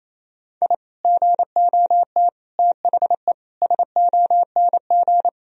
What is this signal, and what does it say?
Morse code, heard as a keyed tone, about 28 wpm, 700 Hz. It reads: I GOT THE SONG
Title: Sentences from Top 1000 Words - Review Part 2 - 28wpm